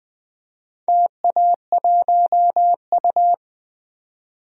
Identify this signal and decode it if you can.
Morse code, heard as a keyed tone, about 20 wpm, 700 Hz. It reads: TA1U
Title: DXCC-100 Call Signs - 4 Characters - 20wpm